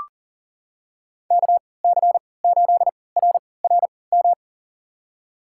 Morse code, heard as a keyed tone, about 37 wpm, spaced as for 22 wpm, 700 Hz. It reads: KC8RRM
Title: USA Call Signs - 6 Characters - 3x Character Spacing - 40wpm